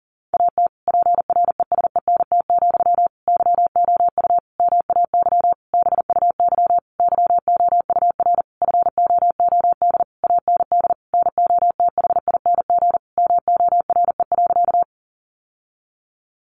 Morse code, heard as a keyed tone, about 40 wpm, 700 Hz. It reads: AT PRESENT, YOU MAY BUY YOUR FOOD AND NOTHING MORE.
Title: Sentences from Top 800 Words 40wpm